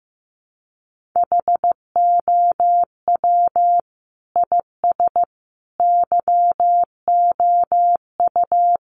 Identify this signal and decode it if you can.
Morse code, heard as a keyed tone, about 15 wpm, 700 Hz. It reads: HOW IS YOU
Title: Sentences from Top 600 Words 15wpm